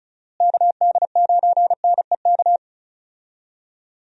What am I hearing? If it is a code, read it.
Morse code, heard as a keyed tone, about 34 wpm, 700 Hz. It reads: KD9NEK
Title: USA Call Signs - 6 Characters - Repeated 3x - 35wpm